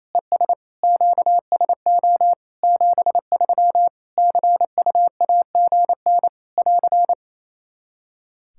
Morse code, heard as a keyed tone, about 28 wpm, 700 Hz. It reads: ES QSO 73 CUAGN <AR>
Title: K1USN SST QSO Practice 28wpm